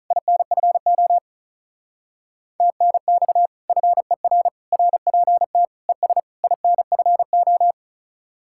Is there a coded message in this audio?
Yes: Morse code from 0.1 to 7.7 s, heard as a keyed tone, about 46 wpm, 700 Hz. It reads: INFO TNX FER RPT ES INFO